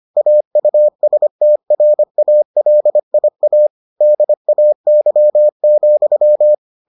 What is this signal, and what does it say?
Morse code, heard as a keyed tone, about 25 wpm, 600 Hz. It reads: AUSTRALIA DAY,